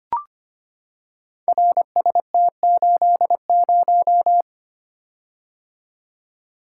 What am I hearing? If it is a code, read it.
Morse code, heard as a keyed tone, about 25 wpm, 700 Hz. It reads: RST80
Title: Code Groups - 5 Letter:Numbers - 25wpm